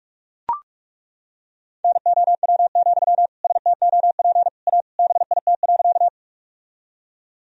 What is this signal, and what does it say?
Morse code, heard as a keyed tone, about 41 wpm, 700 Hz. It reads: NOW, STOP A BIT.
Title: Sentences from Top 1000 Words - Review Part 1 - 45wpm